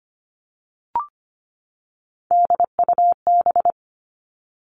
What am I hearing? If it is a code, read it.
Morse code, heard as a keyed tone, about 25 wpm, 700 Hz. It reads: DUB
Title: Three Challenging Letter-Number Combinations 25wpm